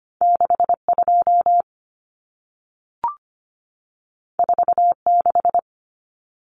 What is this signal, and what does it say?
Morse code, heard as a keyed tone, about 25 wpm, 700 Hz. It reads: 62 46
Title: Numbers - 2-Digit 25wpm